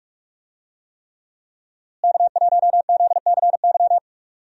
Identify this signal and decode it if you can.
Morse code, heard as a keyed tone, about 44 wpm, 700 Hz. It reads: K1ZCY
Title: USA Call Signs - 5 Characters - Repeated 3x - 45wpm